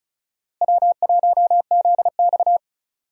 Morse code, heard as a keyed tone, about 34 wpm, 700 Hz. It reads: W1ZX
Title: USA Call Signs - 4 Characters - Repeated 2x - 35wpm